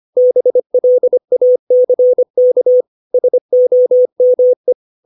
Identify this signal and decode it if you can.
Morse code, heard as a keyed tone, about 25 wpm, 500 Hz. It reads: BLACK SOME
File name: 632 words / 632 Words 25WPM